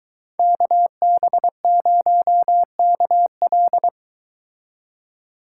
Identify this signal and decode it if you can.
Morse code, heard as a keyed tone, about 22 wpm, 700 Hz. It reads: KB0KL